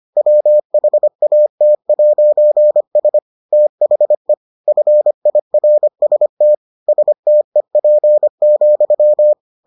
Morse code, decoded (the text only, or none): WHAT'S THE FIRST STEP,